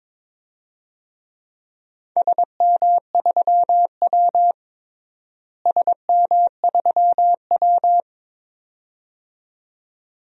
Morse code, heard as a keyed tone, about 22 wpm, 700 Hz. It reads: SM3W SM3W
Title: DXCC-100 Call Signs - 4 Characters - Repeated 2x - 22wpm